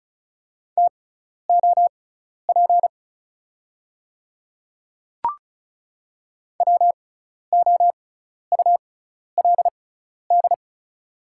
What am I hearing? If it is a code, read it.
Morse code, heard as a keyed tone, about 36 wpm, spaced as for 12 wpm, 700 Hz. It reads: TOP WOULD
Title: 6x Character Spacing - Top 500 Words - 35wpm